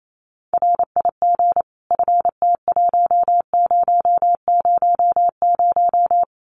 Morse code, heard as a keyed tone, about 28 wpm, 700 Hz. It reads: RIG FT1000
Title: Common QSO Phrases 28wpm